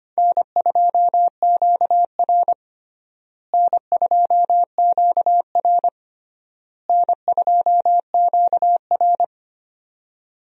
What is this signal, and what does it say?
Morse code, heard as a keyed tone, about 25 wpm, 700 Hz. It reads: N2QR N2QR N2QR
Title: USA Call Signs - 4 Characters - Repeated 3x - 25wpm